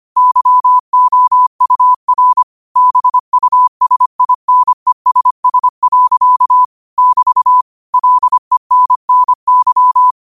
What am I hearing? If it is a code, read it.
Morse code, heard as a keyed tone, about 25 wpm, 1000 Hz. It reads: YOUR BUSINESS. <BT> LENNY
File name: qotdfiles / 25wpm